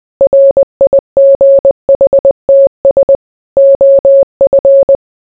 Morse code, heard as a keyed tone, about 20 wpm, 550 Hz. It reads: RIGHTS OF